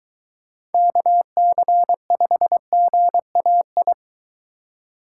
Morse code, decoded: KC5GAI

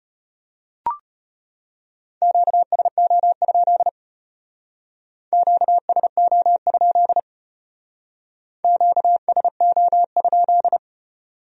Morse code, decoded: QSO? QSO? QSO?

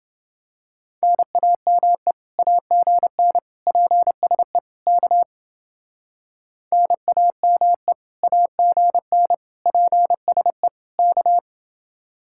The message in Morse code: NAME AGN PSE K NAME AGN PSE K

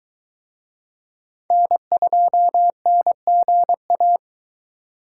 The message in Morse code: N2NGA

